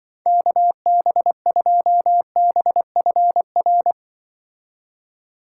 Morse code, heard as a keyed tone, about 25 wpm, 700 Hz. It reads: KB2BFR
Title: Speed Racing - USA 2x3 Call Signs 20wpm